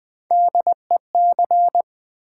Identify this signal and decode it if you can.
Morse code, heard as a keyed tone, about 20 wpm, 700 Hz. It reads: DEC
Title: Build A Word - 3 Letters At A Time - 6-Letter Words - 20wpm